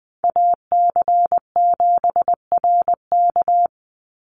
Morse code, heard as a keyed tone, about 20 wpm, 700 Hz. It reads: AC7RK
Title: USA Call Signs 20wpm